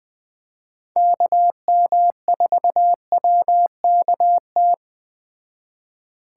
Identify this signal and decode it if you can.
Morse code, heard as a keyed tone, about 21 wpm, 700 Hz. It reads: KM4WKT
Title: Speed Racing - USA Call Signs 17wpm